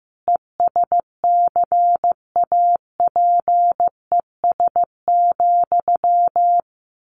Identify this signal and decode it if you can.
Morse code, heard as a keyed tone, about 15 wpm, 700 Hz. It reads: ESCAPES,
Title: A Sound Of Thunder 15wpm